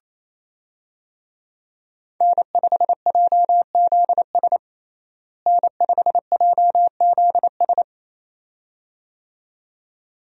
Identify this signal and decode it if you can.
Morse code, heard as a keyed tone, about 28 wpm, 700 Hz. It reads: N5JZS N5JZS
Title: USA Call Signs - 5 Characters - Repeated 2x - 28wpm